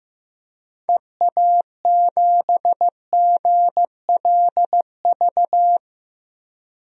Morse code, heard as a keyed tone, about 15 wpm, 700 Hz. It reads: EA7GLV
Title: DXCC-100 Call Signs - 6 Characters - 15wpm